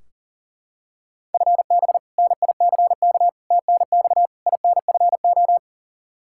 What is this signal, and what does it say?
Morse code, heard as a keyed tone, about 40 wpm, 700 Hz. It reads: FB NICK TNX INFO
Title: Practice QSO - Exchange 3 - More Info - 40wpm